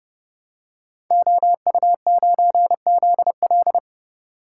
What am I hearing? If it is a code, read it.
Morse code, heard as a keyed tone, about 30 wpm, 700 Hz. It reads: OU9ZL
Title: Code Groups - 5 Letter:Numbers - 30wpm